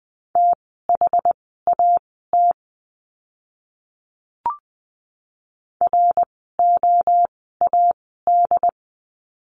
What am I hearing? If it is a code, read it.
Morse code, heard as a keyed tone, about 20 wpm, spaced as for 15 wpm, 700 Hz. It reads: THAT ROAD
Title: 068 Lesson 12f - Letter d - Words 20wpm